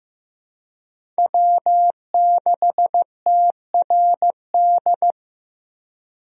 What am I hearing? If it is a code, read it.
Morse code, heard as a keyed tone, about 15 wpm, 700 Hz. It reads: W6TRD
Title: USA Call Signs - 5 Characters - 15wpm